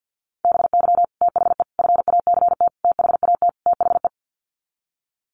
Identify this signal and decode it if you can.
Morse code, heard as a keyed tone, about 50 wpm, 700 Hz. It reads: BY THE FACT THAT THE
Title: Sets of 5 Words Encore 50wpm